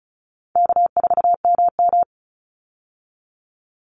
Morse code, heard as a keyed tone, about 34 wpm, 700 Hz. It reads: K4MM
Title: Easy Call Signs Repeated 2x - 35wpm